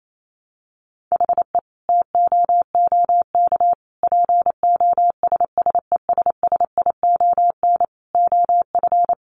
Morse code, heard as a keyed tone, about 28 wpm, 700 Hz. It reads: HE TOOK POSSESSION OF
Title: Sentences from Top 1500 Words 28wpm